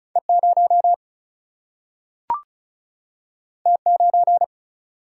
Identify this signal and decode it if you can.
Morse code, heard as a keyed tone, about 36 wpm, 700 Hz. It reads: E0 T9
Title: Single-Letter Single-Number 35wpm